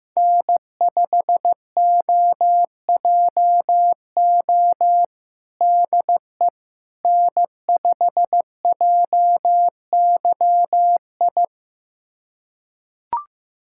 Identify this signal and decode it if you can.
Morse code, heard as a keyed tone, about 15 wpm, 700 Hz. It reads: N5OJO DE N5JYI
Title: USA Callsign DE Callsign - 5 Characters - 15wpm